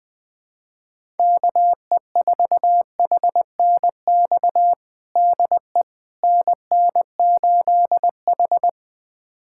Morse code, heard as a keyed tone, about 20 wpm, 700 Hz. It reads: KE4HNX DE NN8H
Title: USA Callsign DE Callsign 20wpm